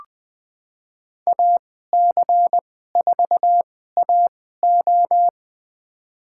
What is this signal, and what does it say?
Morse code, heard as a keyed tone, about 20 wpm, spaced as for 14 wpm, 700 Hz. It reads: AC4AO
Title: USA Call Signs - 5 Characters - 2x Character Spacing - 20wpm